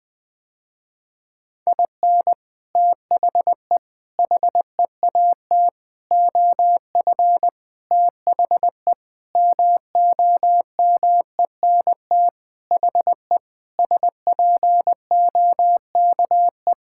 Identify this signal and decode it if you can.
Morse code, heard as a keyed tone, about 21 wpm, 700 Hz. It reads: IN THE HEAT OF THE MOMENT HE SPOKE
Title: Sentences with 6-Word Phrases - Spoken Beginning - 20wpm